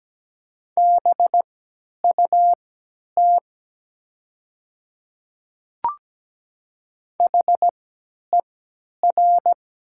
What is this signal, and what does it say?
Morse code, heard as a keyed tone, about 17 wpm, spaced as for 10 wpm, 700 Hz. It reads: BUT HER
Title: 3x Character Spacing - Top 100 Words - 17wpm